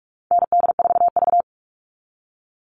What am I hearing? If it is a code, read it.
Morse code, heard as a keyed tone, about 44 wpm, 700 Hz. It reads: ND4V
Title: Call Signs Repeated 3x - 45wpm